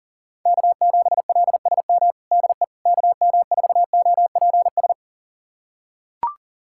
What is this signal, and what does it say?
Morse code, heard as a keyed tone, about 40 wpm, 700 Hz. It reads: K7LSM DE KM4OPS
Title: USA Callsign DE Callsign 40wpm